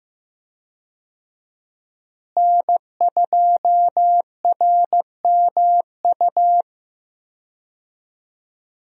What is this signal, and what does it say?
Morse code, heard as a keyed tone, about 15 wpm, 700 Hz. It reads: N2RMU
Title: USA Call Signs - 5 Characters - Repeated 2x - 15wpm